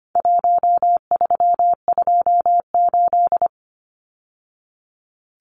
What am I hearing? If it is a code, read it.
Morse code, heard as a keyed tone, about 25 wpm, 700 Hz. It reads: 1328
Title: Numbers - 4-Digit 25wpm